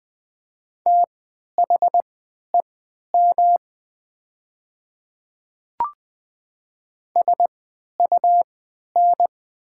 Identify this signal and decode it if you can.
Morse code, heard as a keyed tone, about 20 wpm, spaced as for 11 wpm, 700 Hz. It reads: THEM SUN
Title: 3x Character Spacing - Top 500 Words - 20wpm